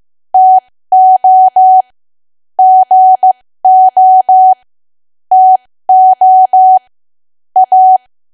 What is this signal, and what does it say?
Morse code, heard as a keyed tone, about 15 wpm, spaced as for 13 wpm, 750 Hz. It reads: TO GO TO A
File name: cwa / W1AW-Level2 / 13WPM